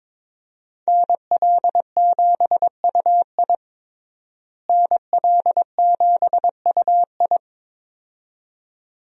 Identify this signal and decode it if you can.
Morse code, heard as a keyed tone, about 22 wpm, 700 Hz. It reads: NL7UI NL7UI
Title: USA Call Signs - 5 Characters - Repeated 2x - 22wpm